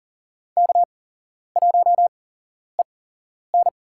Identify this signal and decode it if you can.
Morse code, heard as a keyed tone, about 38 wpm, spaced as for 11 wpm, 700 Hz. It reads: K1EN
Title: USA Call Signs - 4 Characters - 8x Character Spacing - 40wpm